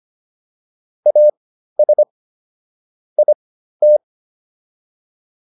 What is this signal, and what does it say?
Morse code, heard as a keyed tone, about 25 wpm, spaced as for 13 wpm, 600 Hz. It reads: AS IT